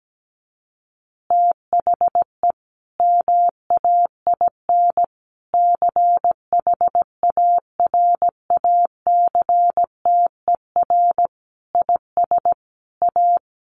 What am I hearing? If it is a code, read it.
Morse code, heard as a keyed tone, about 17 wpm, 700 Hz. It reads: THE MAIN CHARACTER IS A
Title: Additional Easy Phrases 17wpm